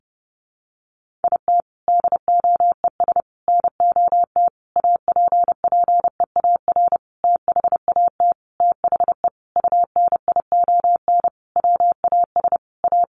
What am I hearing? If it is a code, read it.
Morse code, heard as a keyed tone, about 30 wpm, 700 Hz. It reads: IT DOES NOT APPEAR THAT THE UNION WAS A
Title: Sentences from Top 1400 Words 30wpm